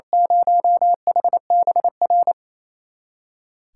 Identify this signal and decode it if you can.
Morse code, heard as a keyed tone, about 28 wpm, 700 Hz. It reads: 0HBR
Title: USA Callsign DE Callsign - 6 Characters - 28wpm